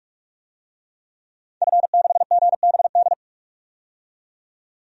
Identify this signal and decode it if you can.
Morse code, heard as a keyed tone, about 43 wpm, 700 Hz. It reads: F6GBD
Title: DXCC-100 Call Signs - 5 Characters - 45wpm